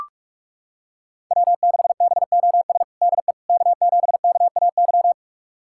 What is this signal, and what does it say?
Morse code, heard as a keyed tone, about 43 wpm, 700 Hz. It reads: W6BOS DE K7KAY
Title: USA Callsign DE Callsign - 5 Characters - 45wpm